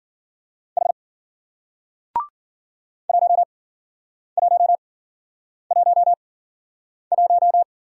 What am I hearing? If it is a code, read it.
Morse code, heard as a keyed tone, about 51 wpm, spaced as for 9 wpm, 700 Hz. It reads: H 1111